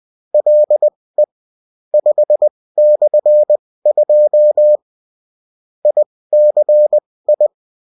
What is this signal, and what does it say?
Morse code, heard as a keyed tone, about 20 wpm, spaced as for 16 wpm, 600 Hz. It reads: LE 5/2 ICI
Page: bulletin-uft-sous-forme-texte-et-en-cw